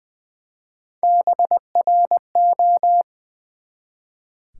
Morse code, heard as a keyed tone, about 20 wpm, 700 Hz. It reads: BRO
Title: Build A Word - 1 Letter At A Time - 5-Letter Words - 20wpm